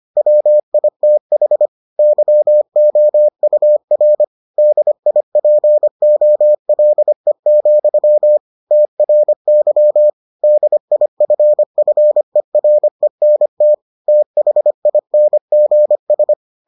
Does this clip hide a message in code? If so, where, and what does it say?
Yes: Morse code from 0.2 to 16.3 s, heard as a keyed tone, about 25 wpm, 600 Hz. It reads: WITH YOUR DIPOLE, TRY DIFFERENT THINGS